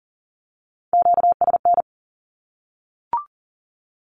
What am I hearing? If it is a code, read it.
Morse code, heard as a keyed tone, about 40 wpm, 700 Hz. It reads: QSN
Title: Q-Codes Repeated 3x 40wpm